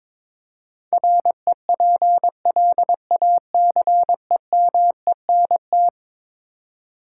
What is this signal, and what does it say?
Morse code, heard as a keyed tone, about 22 wpm, 700 Hz. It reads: REPLACEMENT
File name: practice / 22wpm / Eleven Letter Words - 22wpm